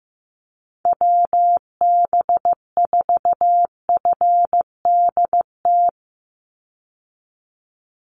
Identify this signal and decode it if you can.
Morse code, heard as a keyed tone, about 15 wpm, 700 Hz. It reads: WB4FDT
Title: Call Signs Repeated 2x - 15wpm